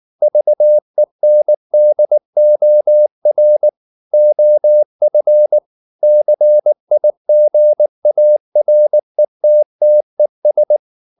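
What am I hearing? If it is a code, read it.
Morse code, heard as a keyed tone, about 19 wpm, 600 Hz. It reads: VENDOR OF CIGARETTES
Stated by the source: Morse code at 19 wpm from CW med Gnister, reading Ken Follett's novel Never in English.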